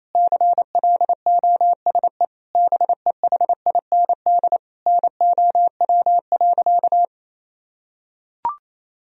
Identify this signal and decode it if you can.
Morse code, heard as a keyed tone, about 28 wpm, 700 Hz. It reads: CLOSE BEHIND NOW.
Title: Sentences from Top 1000 Words - Review Part 1 - 28wpm